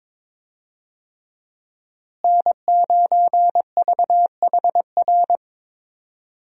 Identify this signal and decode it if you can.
Morse code, heard as a keyed tone, about 22 wpm, 700 Hz. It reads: N9VHR
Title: USA Call Signs - 5 Characters - Repeated 3x - 22wpm